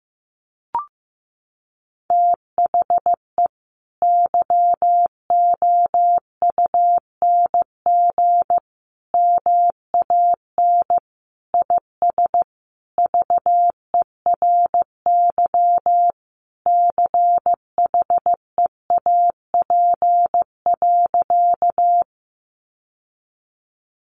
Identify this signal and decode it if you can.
Morse code, heard as a keyed tone, about 15 wpm, 700 Hz. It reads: THE YOUNG MAN IS VERY CHEAP.
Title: Sentences from Top 1200 Words 15wpm